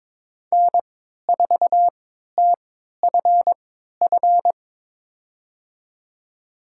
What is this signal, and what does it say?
Morse code, heard as a keyed tone, about 22 wpm, spaced as for 12 wpm, 700 Hz. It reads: N4TFF
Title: USA Call Signs - 5 Characters - 3x Character Spacing - 22wpm